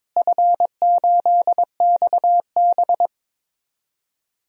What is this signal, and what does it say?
Morse code, heard as a keyed tone, about 22 wpm, 700 Hz. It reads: F8XB
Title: DXCC-100 Call Signs - 4 Characters - Repeated 3x - 22wpm